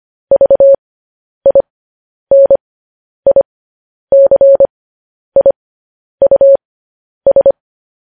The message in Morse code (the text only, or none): VINICIUS